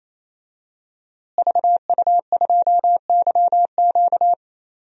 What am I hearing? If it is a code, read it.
Morse code, heard as a keyed tone, about 27 wpm, 700 Hz. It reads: VU2YQ
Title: DXCC-100 Call Signs - 5 Characters - Repeated 3x - 28wpm